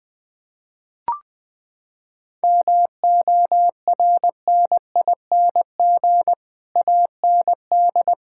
Morse code, heard as a keyed tone, about 20 wpm, 700 Hz. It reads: MORNING AND
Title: Sets of 9 Words - 20wpm